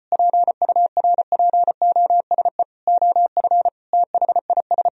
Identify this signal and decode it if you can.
Morse code, heard as a keyed tone, about 33 wpm, 700 Hz. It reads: PURPOSE OF THIS